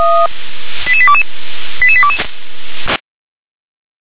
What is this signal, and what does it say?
Five-tone selective call (ZVEI), heard as quick runs of five tones, tones of about 70 ms. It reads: 88822 88822
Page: SDR/ZVEI/88810